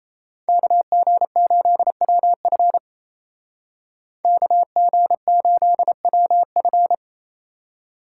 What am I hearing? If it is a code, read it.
Morse code, heard as a keyed tone, about 36 wpm, 700 Hz. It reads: KG8WF KG8WF